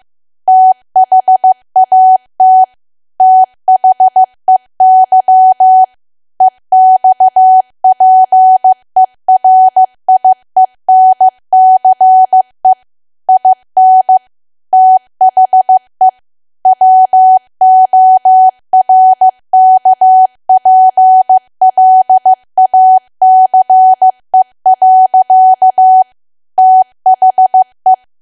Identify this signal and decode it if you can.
Morse code, heard as a keyed tone, about 15 wpm, 750 Hz. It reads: THAT THEY EXPERIENCE IN THE WORKPLACE. THE
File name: cwa / W1AW-Level2 / 15WPM